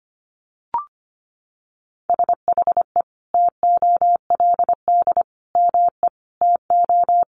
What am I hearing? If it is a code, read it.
Morse code, heard as a keyed tone, about 25 wpm, 700 Hz. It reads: SHE TOLD ME TO
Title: Sentences from Top 400 Words 25wpm